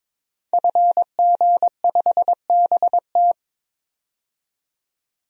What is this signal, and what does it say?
Morse code, heard as a keyed tone, about 22 wpm, 700 Hz. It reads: FG5BT
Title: DXCC-100 Call Signs - 5 Characters - 22wpm